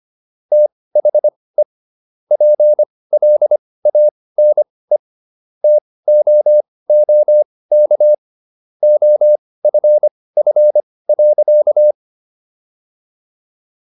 Morse code, heard as a keyed tone, about 25 wpm, spaced as for 18 wpm, 600 Hz. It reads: THE PLANE TOOK OFF.